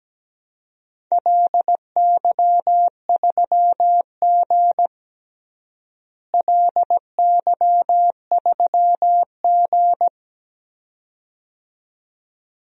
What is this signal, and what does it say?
Morse code, heard as a keyed tone, about 17 wpm, 700 Hz. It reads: LY3G LY3G
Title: DXCC-100 Call Signs - 4 Characters - Repeated 2x - 17wpm